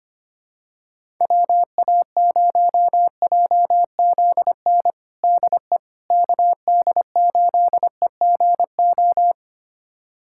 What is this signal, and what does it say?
Morse code, heard as a keyed tone, about 25 wpm, 700 Hz. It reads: WA0JZN DE KD8EGO
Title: USA Callsign DE Callsign 25wpm